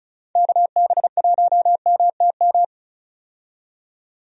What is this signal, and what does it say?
Morse code, heard as a keyed tone, about 36 wpm, 700 Hz. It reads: KB1MTM
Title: USA Call Signs - 6 Characters - 35wpm